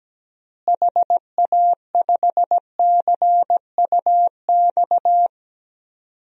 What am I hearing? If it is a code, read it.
Morse code, heard as a keyed tone, about 17 wpm, 700 Hz. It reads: HA5CUX
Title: DXCC-100 Call Signs - 6 Characters - Repeated 2x - 17wpm